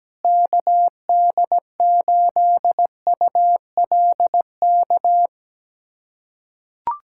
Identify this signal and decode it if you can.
Morse code, heard as a keyed tone, about 17 wpm, 700 Hz. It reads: KD8ULK
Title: USA Callsign DE Callsign 17wpm